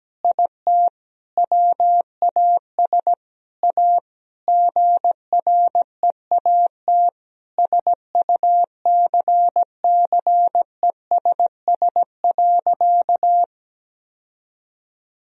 Morse code, heard as a keyed tone, about 17 wpm, 700 Hz. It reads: IT WAS A GREAT SUCCESS.
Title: Sentences from Top 1000 Words - Review Part 4 - 17wpm